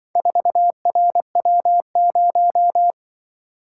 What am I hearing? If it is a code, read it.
Morse code, heard as a keyed tone, about 23 wpm, 700 Hz. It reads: 4RW0
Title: Speed-Racing - Four Letter-Number Combinations - 20wpm